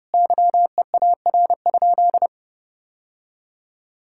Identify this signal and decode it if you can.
Morse code, heard as a keyed tone, about 32 wpm, 700 Hz. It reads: YEAR?
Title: Sentences from Top 1000 Words - Review Part 5 - 30wpm